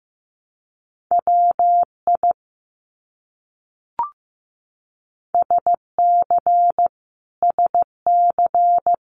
Morse code, heard as a keyed tone, about 15 wpm, 700 Hz. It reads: WI SC SC
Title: US States Repeated 3x 15wpm